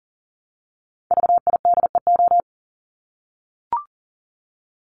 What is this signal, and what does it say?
Morse code, heard as a keyed tone, about 40 wpm, 700 Hz. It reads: VIDEO
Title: Top 501-750 Words 40wpm